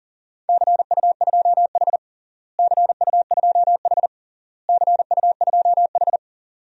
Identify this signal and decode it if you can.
Morse code, heard as a keyed tone, about 40 wpm, 700 Hz. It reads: CU2H CU2H CU2H
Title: DXCC-100 Call Signs - 4 Characters - Repeated 3x - 40wpm